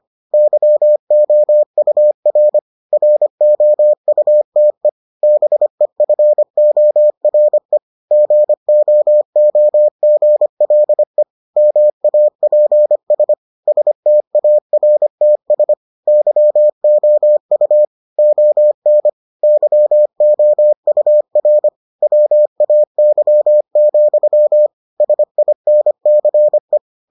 Morse code, heard as a keyed tone, about 25 wpm, 600 Hz. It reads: YOUR ROUTE BEFORE GOOGLE MAPS STARTS YOU ON YOUR WAY, SINCE